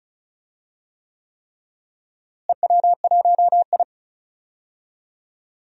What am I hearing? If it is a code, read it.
Morse code, heard as a keyed tone, about 36 wpm, 700 Hz. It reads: EW1I
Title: DXCC-100 Call Signs - 4 Characters - Repeated 3x - 35wpm